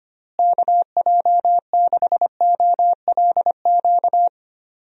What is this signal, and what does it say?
Morse code, heard as a keyed tone, about 25 wpm, 700 Hz. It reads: KJ6OLQ
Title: USA Call Signs - 6 Characters - Repeated 2x - 25wpm